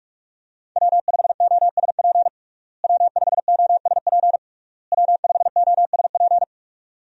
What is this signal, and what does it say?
Morse code, heard as a keyed tone, about 46 wpm, 700 Hz. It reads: W5OSP W5OSP W5OSP
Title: USA Call Signs - 5 Characters - Repeated 3x - 45wpm